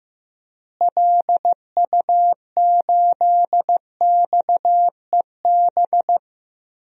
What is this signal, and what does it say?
Morse code, heard as a keyed tone, about 16 wpm, 700 Hz. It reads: LU8XEB
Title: DXCC-100 Call Signs - 6 Characters - Repeated 2x - 15wpm